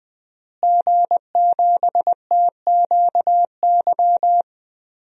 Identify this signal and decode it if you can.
Morse code, heard as a keyed tone, about 21 wpm, 700 Hz. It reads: G7TQY
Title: DXCC-100 Call Signs - 5 Characters - Repeated 3x - 20wpm